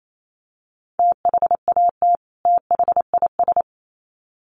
Morse code, heard as a keyed tone, about 28 wpm, 700 Hz. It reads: THAT THIS
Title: Sets of 2 Words 28wpm